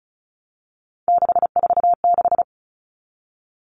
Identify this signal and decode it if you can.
Morse code, heard as a keyed tone, about 34 wpm, 700 Hz. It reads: 646
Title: Numbers - 3-Digit 35wpm